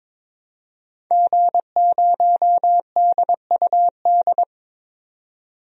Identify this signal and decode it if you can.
Morse code, heard as a keyed tone, about 23 wpm, 700 Hz. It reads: G0DUD
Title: DXCC-100 Call Signs - 5 Characters - 22wpm